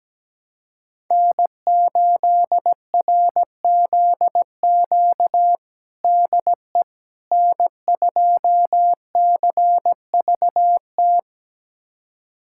Morse code, heard as a keyed tone, about 17 wpm, 700 Hz. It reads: N8RZQ DE N2CVT
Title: USA Callsign DE Callsign - 5 Characters - 17wpm